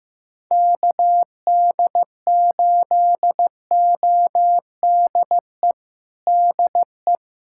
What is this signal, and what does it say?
Morse code, heard as a keyed tone, about 15 wpm, 700 Hz. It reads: KD8ODE DE
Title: USA Callsign DE Callsign - 6 Characters - 15wpm